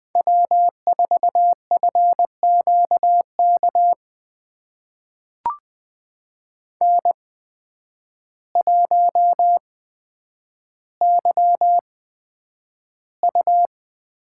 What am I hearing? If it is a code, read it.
Morse code, heard as a keyed tone, about 20 wpm, spaced as for 9 wpm, 700 Hz. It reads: W4FQK N1YU
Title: USA Call Signs - 5 Characters - 8x Character Spacing - 20wpm